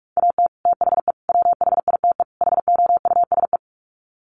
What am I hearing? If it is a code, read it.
Morse code, heard as a keyed tone, about 44 wpm, 700 Hz. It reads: AT THE WHITE HOUSE
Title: Sets of 4 Words 45wpm